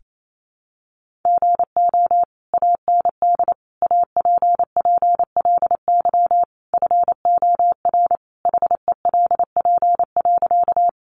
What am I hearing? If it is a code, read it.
Morse code, heard as a keyed tone, about 28 wpm, 700 Hz. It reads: GO AND APPLY FOR HELP.
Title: Sentences from Top 1200 Words 28wpm